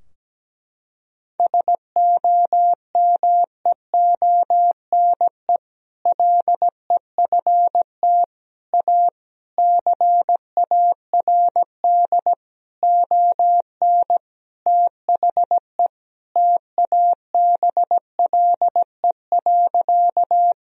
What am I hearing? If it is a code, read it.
Morse code, heard as a keyed tone, about 17 wpm, 700 Hz. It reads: SOMEONE LEFT A CARD ON THE TABLE.
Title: Sentences from Top 1000 Words - Review Part 4 - 17wpm